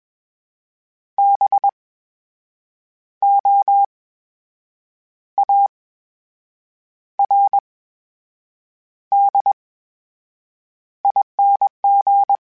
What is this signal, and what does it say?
Morse code, heard as a keyed tone, about 21 wpm, spaced as for 7 wpm, 800 Hz. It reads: BOARDING
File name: Special INGs-mp3s / Special INGs-mp3s / VerySlowToFaster Ing @20